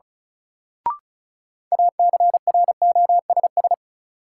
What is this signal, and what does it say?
Morse code, heard as a keyed tone, about 36 wpm, 700 Hz. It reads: ACROSS
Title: Sets of 2 Words - New Word Drill - 35wpm